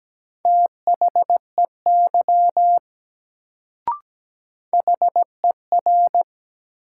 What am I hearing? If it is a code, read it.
Morse code, heard as a keyed tone, about 17 wpm, 700 Hz. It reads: THEY HER
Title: Sentences from Top 100 Words - New Word Drill - 17wpm